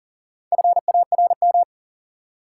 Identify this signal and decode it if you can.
Morse code, heard as a keyed tone, about 40 wpm, 700 Hz. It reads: FARM
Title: Sentences from Top 1000 Words - Review Part 2 - New Word Drill - 40wpm